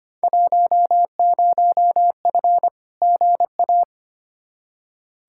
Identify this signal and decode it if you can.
Morse code, heard as a keyed tone, about 25 wpm, 700 Hz. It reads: 10F GA
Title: Field Day Exchanges Encore - 25wpm